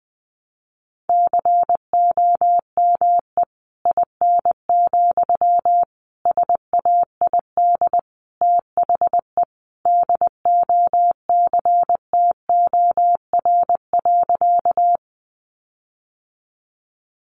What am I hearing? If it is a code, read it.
Morse code, heard as a keyed tone, about 20 wpm, 700 Hz. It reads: COME IN, SAID THE DOCTOR.
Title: Sentences from Top 1500 Words 20wpm